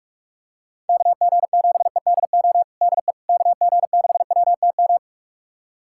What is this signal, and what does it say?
Morse code, heard as a keyed tone, about 46 wpm, 690 Hz. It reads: KG7EDO DE KG6WTM
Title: USA Callsign DE Callsign 45wpm